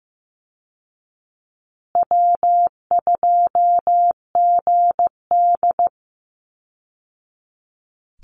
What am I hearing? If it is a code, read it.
Morse code, heard as a keyed tone, about 15 wpm, 700 Hz. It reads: W2GD
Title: Easy Call Signs Repeated 3x - 15wpm